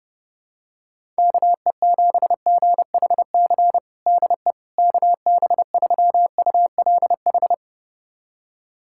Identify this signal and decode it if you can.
Morse code, heard as a keyed tone, about 30 wpm, 700 Hz. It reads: KE7GHC DE KB3ULH
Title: USA Callsign DE Callsign - 6 Characters - 30wpm